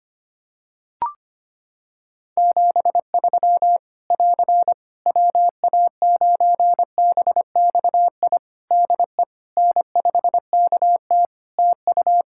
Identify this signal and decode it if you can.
Morse code, heard as a keyed tone, about 25 wpm, 700 Hz. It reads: 73 <AR> WA9BXI DE N5KT TU
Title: Practice QSO - Exchange 4 - Ending - 25wpm